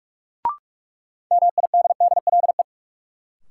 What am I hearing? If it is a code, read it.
Morse code, heard as a keyed tone, about 46 wpm, 700 Hz. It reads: MIDDLE
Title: Sentences from Top 900 Words - New Word Drill - 45wpm